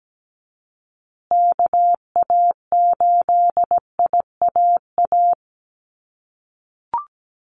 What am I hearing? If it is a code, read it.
Morse code, heard as a keyed tone, about 17 wpm, 700 Hz. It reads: KA8IAA
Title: USA Call Signs Repeated 3x 17wpm